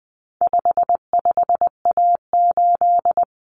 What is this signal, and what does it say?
Morse code, heard as a keyed tone, about 20 wpm, 700 Hz. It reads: 55A8
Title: Common Cut Numbers - 4-Digit 20wpm